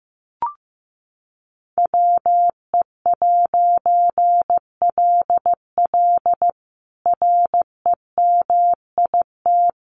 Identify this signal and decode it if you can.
Morse code, heard as a keyed tone, about 15 wpm, 700 Hz. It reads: WE'LL REMIT
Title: A Sound Of Thunder 15wpm